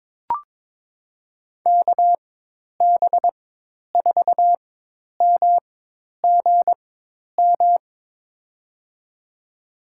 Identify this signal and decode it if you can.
Morse code, heard as a keyed tone, about 22 wpm, spaced as for 10 wpm, 700 Hz. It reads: KB4MGM
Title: USA Call Signs - 6 Characters - 4x Character Spacing - 22wpm